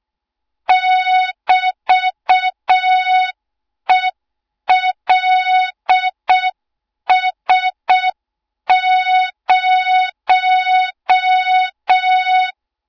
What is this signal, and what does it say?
Morse code, heard as a keyed tone, about 6 wpm, 3700 Hz. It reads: <BT>ELS0
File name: Morze-HA2MN / hang / 20 GYAKORLAT